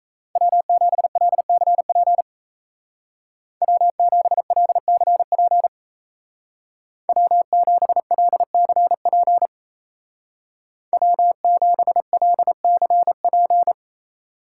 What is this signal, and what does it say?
Morse code, heard as a keyed tone, about 33 wpm, 700 Hz. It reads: W7LCP W7LCP W7LCP W7LCP
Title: Speed Racing - USA 1x3 Call Signs 28wpm